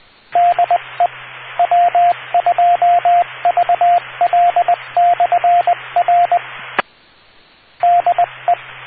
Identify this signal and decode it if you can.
Morse code, heard as a keyed tone, about 20 wpm, 700 Hz, with radio noise. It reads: DE W2VL/R DE